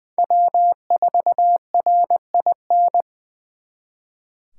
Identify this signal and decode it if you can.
Morse code, heard as a keyed tone, about 24 wpm, 700 Hz. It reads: W4RIN
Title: Speed Racing - USA 1x3 Call Signs 20wpm